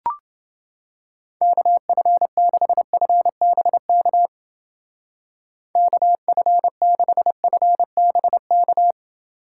Morse code, heard as a keyed tone, about 30 wpm, 700 Hz. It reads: KF6FBK KF6FBK